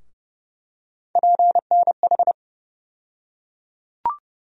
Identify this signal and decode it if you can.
Morse code, heard as a keyed tone, about 30 wpm, 700 Hz. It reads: PNH
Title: USA Call Sign Postfixes - 3 Letters 30wpm